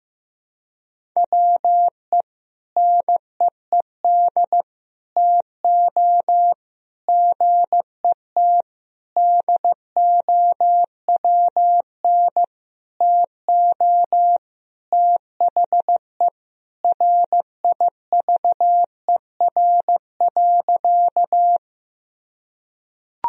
Morse code, heard as a keyed tone, about 16 wpm, 700 Hz. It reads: WE NEED TO GET DOWN TO THE RIVER.
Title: Sentences from Top 1000 Words - Review Part 1 - 15wpm